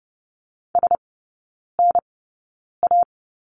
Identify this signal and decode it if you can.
Morse code, heard as a keyed tone, about 30 wpm, spaced as for 9 wpm, 700 Hz. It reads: SNA
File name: Lesson 15e - Number 5 - Three Characters 30wpm